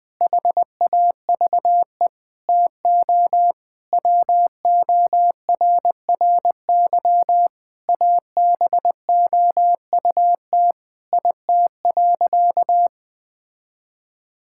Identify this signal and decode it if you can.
Morse code, heard as a keyed tone, about 20 wpm, 700 Hz. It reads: HAVE TO WORRY ABOUT IT.
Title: Sentences with 5-Word Phrases - 20wpm